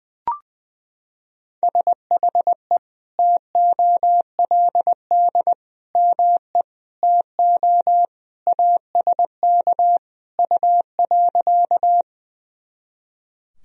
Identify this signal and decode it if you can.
Morse code, heard as a keyed tone, about 19 wpm, 700 Hz. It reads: SHE TOLD ME TO ASK U.
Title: Sentences from Top 400 Words and Abbreviations 20wpm